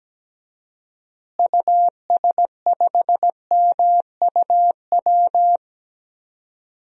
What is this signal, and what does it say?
Morse code, heard as a keyed tone, about 17 wpm, 700 Hz. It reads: US5MUW
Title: DXCC-100 Call Signs - 6 Characters - Repeated 3x - 17wpm